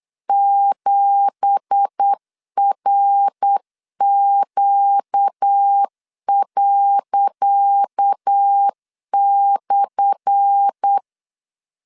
Morse code, heard as a keyed tone, about 8 wpm, 800 Hz. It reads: 7RQ./
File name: LEZIONI / 10 SIGNS / 10C DOT 8wpm 3649